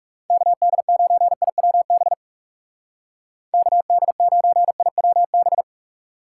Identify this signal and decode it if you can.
Morse code, heard as a keyed tone, about 37 wpm, 700 Hz. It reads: KD9IWB KD9IWB
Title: Speed Racing - USA Call Signs 30wpm